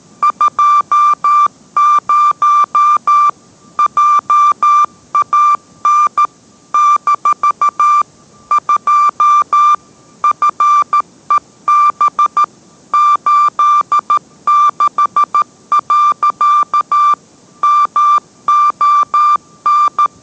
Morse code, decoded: 20 JAN - 2 FEB 86. MON

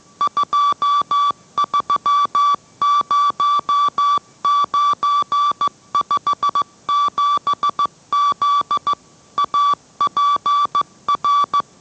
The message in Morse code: 230957Z APR

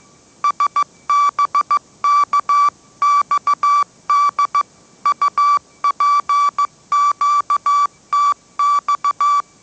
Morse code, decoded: SBKXD UPQTX

